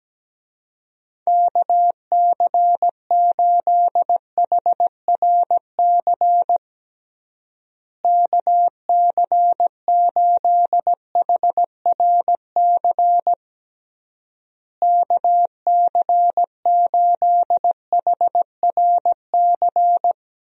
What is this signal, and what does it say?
Morse code, heard as a keyed tone, about 17 wpm, 700 Hz. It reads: KC8HRC KC8HRC KC8HRC
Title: USA Call Signs - 6 Characters - Repeated 3x - 17wpm